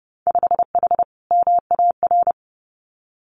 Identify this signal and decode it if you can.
Morse code, heard as a keyed tone, about 30 wpm, 700 Hz. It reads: 5H MAR
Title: Winter Field Day Exchanges Encore 30wpm